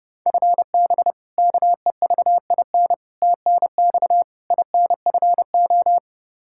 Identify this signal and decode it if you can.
Morse code, heard as a keyed tone, about 30 wpm, 700 Hz. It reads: FB KEVIN TNX INFO
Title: Practice QSO - Exchange 3 - More Info - 30wpm